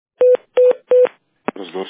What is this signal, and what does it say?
Speech. There is no code.